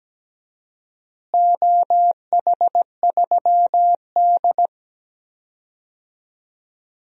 Morse code, heard as a keyed tone, about 17 wpm, 700 Hz. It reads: OH3D
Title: DXCC-100 Call Signs - 4 Characters - Repeated 2x - 17wpm